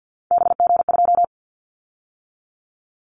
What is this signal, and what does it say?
Morse code, heard as a keyed tone, about 50 wpm, 690 Hz. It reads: BG2
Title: Callsign Prefixes for Top 100 DXCC - 50wpm